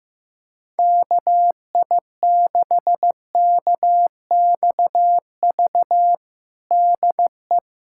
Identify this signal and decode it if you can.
Morse code, heard as a keyed tone, about 15 wpm, 700 Hz. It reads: KI6KXV DE
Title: USA Callsign DE Callsign 15wpm